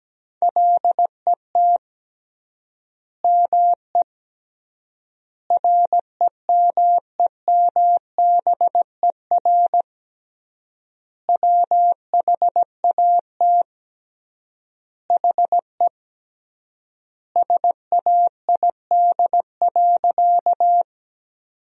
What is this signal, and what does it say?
Morse code, heard as a keyed tone, about 17 wpm, 700 Hz. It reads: LET ME REMEMBER WHAT HE SAID.
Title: 3x Word-Spacing - Sentences from Top 400 Words - 17wpm